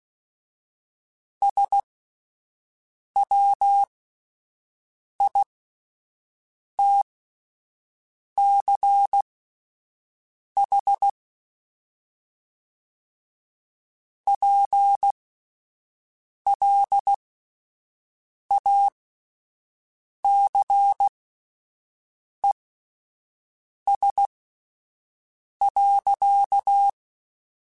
Morse code, heard as a keyed tone, about 16 wpm, spaced as for 5 wpm, 800 Hz. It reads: SWITCH PLACES.